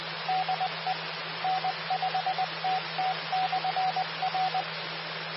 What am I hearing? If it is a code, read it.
Morse code, heard as a keyed tone, about 20 wpm, 750 Hz. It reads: DE N5TT/R
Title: Austin, TX Amateur Repeaters